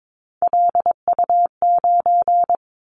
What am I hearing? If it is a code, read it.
Morse code, heard as a keyed tone, about 22 wpm, 700 Hz. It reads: LU9
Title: Callsign Prefixes for Top 100 DXCC - 22wpm